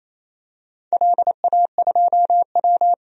Morse code, heard as a keyed tone, about 27 wpm, 690 Hz. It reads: LA2W
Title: DXCC-100 Call Signs - 4 Characters - 28wpm